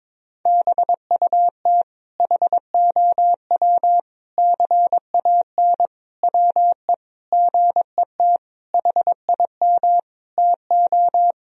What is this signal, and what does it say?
Morse code, heard as a keyed tone, about 21 wpm, 700 Hz. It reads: BUT HOW CAN WE GET HIM TO
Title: Sentences from Top 1000 Words - Review Part 4 - 22wpm